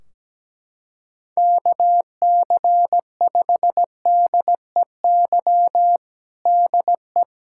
Morse code, heard as a keyed tone, about 17 wpm, 700 Hz. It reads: KC5DEY DE
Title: Practice QSO - Exchange 4 - Ending - 17wpm